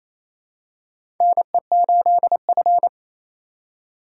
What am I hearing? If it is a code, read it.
Morse code, heard as a keyed tone, about 28 wpm, 700 Hz. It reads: NE8F
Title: USA Call Signs - 4 Characters - 4x Character Spacing - 28wpm